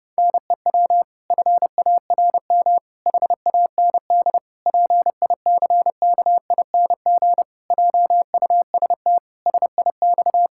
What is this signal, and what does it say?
Morse code, heard as a keyed tone, about 30 wpm, 700 Hz. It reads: NEW FARM HAND PICKING JUST SIX
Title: Pangram Challenge v2 - 30wpm